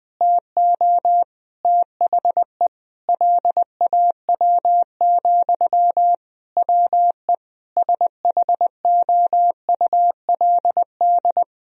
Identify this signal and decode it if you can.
Morse code, heard as a keyed tone, about 20 wpm, 700 Hz. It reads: TO THE LAW, WE SHOULD
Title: Sentences from Top 1000 Words - Review Part 4 - 20wpm